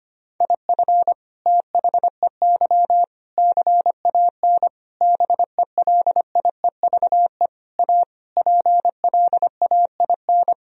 Morse code, heard as a keyed tone, about 25 wpm, 700 Hz. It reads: IF THEY CAN BELIEVE A PLAIN
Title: Sentences from Top 1000 Words - Review Part 2 - 25wpm